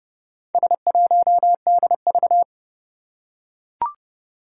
Morse code, heard as a keyed tone, about 30 wpm, 700 Hz. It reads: S1DV
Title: Four Challenging Letter-Number Combinations 30wpm